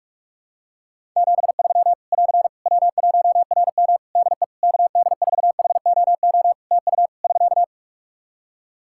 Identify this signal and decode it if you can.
Morse code, heard as a keyed tone, about 46 wpm, 700 Hz. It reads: 73 <AR> W1RM DE KD4HOO TU <SK>
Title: Practice QSO - Exchange 4 - Ending - 45wpm